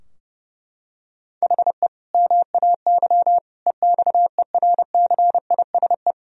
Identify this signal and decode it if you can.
Morse code, heard as a keyed tone, about 32 wpm, 700 Hz. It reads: HE MAY EXERCISE
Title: Sentences from Top 1000 Words - Review Part 4 - 30wpm